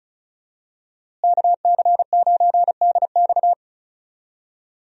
Morse code, heard as a keyed tone, about 34 wpm, 700 Hz. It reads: KC9DX
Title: USA Call Signs - 5 Characters - Repeated 2x - 35wpm